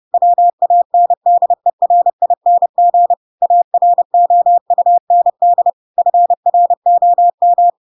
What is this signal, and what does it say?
Morse code, heard as a keyed tone, about 30 wpm, 700 Hz. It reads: WANDERING AROUND FROM
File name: Pinocchio / Pinocchio ch4 30 30 ik1hge